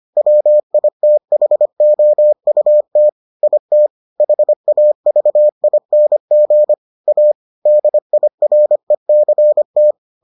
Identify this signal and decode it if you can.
Morse code, heard as a keyed tone, about 25 wpm, 600 Hz. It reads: WITHOUT IT HAVING A DIRECT